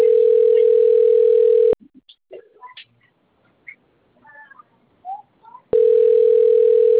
Ringback tone (North America), 440 + 480 Hz.